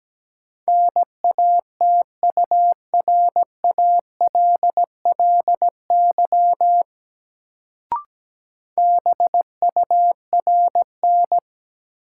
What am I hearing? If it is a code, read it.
Morse code, heard as a keyed tone, about 17 wpm, 700 Hz. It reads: NATURALLY BURN
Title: Sentences from Top 1300 Words - New Word Drill - 17wpm